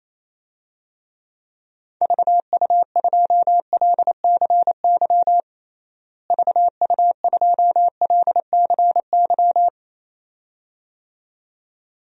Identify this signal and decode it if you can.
Morse code, heard as a keyed tone, about 28 wpm, 700 Hz. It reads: VU2LCY VU2LCY
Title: DXCC-100 Call Signs - 6 Characters - Repeated 2x - 28wpm